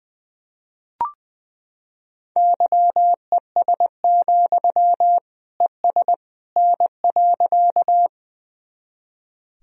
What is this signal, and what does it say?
Morse code, heard as a keyed tone, about 20 wpm, 700 Hz. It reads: YES, ES N.
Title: Sentences from Top 500 Words and Abbreviations 20wpm